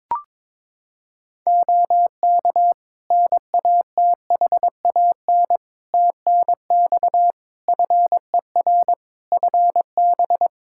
Morse code, heard as a keyed tone, about 22 wpm, 700 Hz. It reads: OK NATHAN TNX FER FB
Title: Practice QSO - Exchange 4 - Ending - 22wpm